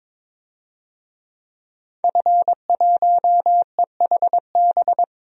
Morse code, heard as a keyed tone, about 22 wpm, 700 Hz. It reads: F1EHB
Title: DXCC-100 Call Signs - 5 Characters - 22wpm